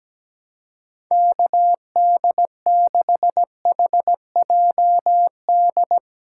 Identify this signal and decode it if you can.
Morse code, heard as a keyed tone, about 17 wpm, 700 Hz. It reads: KD6HJD
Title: USA Call Signs - 6 Characters - 4x Character Spacing - 17wpm